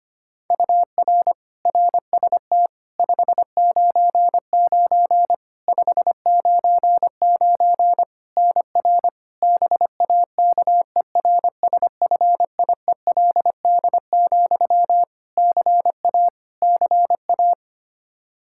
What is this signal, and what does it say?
Morse code, heard as a keyed tone, about 25 wpm, 700 Hz. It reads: UR RST 599 599 NR BAKERSFIELD, CA CA
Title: Practice QSO - Exchange 1 - Beginning - 25wpm